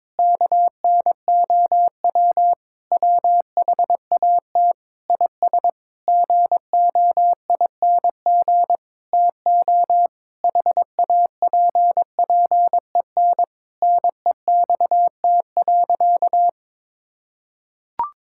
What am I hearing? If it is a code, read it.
Morse code, heard as a keyed tone, about 22 wpm, 700 Hz. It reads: KNOW WHAT IS GOING TO HAPPEN NEXT.
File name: Sentences with 7-Word Phrases - 22wpm